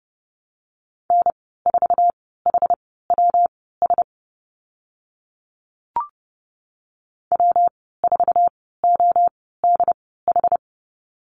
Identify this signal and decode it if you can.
Morse code, heard as a keyed tone, about 30 wpm, spaced as for 17 wpm, 700 Hz. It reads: N4HWS W4ODH
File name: Lesson 19g - Letter w - Call Signs 30wpm